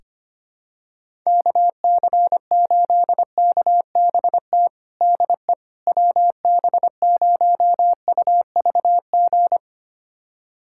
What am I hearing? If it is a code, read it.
Morse code, heard as a keyed tone, about 25 wpm, 700 Hz. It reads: KC8KBT DE WB0UVG
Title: USA Callsign DE Callsign - 6 Characters - 25wpm